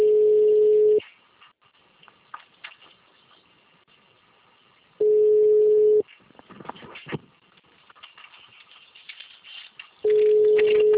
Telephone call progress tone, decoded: ringback tone

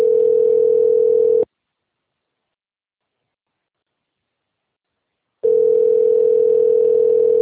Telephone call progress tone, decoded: ringback tone

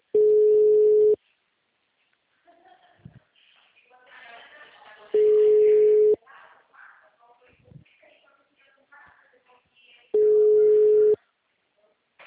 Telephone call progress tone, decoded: ringback tone

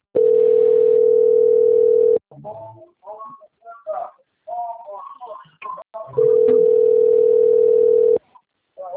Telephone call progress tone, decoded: ringback tone

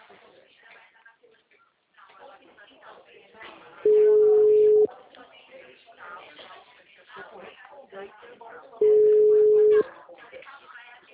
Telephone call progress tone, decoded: ringback tone

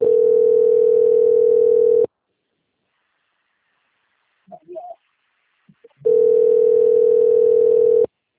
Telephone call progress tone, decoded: ringback tone

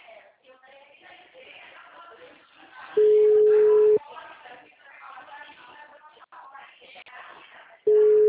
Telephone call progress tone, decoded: ringback tone

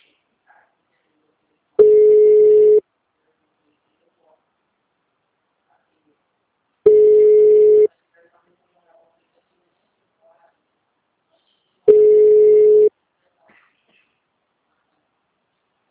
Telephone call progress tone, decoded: ringback tone

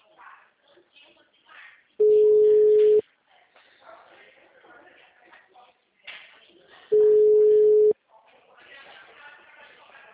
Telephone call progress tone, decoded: ringback tone

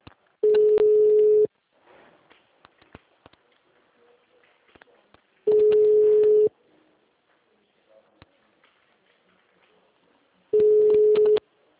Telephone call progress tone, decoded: ringback tone

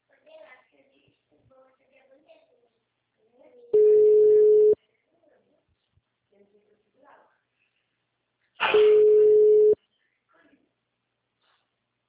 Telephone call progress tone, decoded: ringback tone

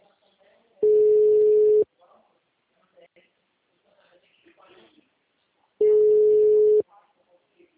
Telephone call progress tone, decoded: ringback tone

